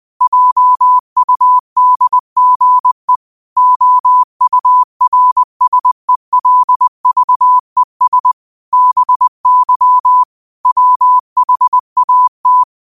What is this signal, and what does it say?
Morse code, heard as a keyed tone, about 20 wpm, 1000 Hz. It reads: JUDGE OURSELVES BY WHAT